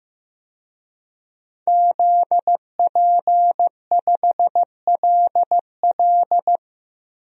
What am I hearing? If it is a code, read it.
Morse code, heard as a keyed tone, about 15 wpm, 700 Hz. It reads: ZP5LL
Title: DXCC-100 Call Signs - 5 Characters - Repeated 3x - 15wpm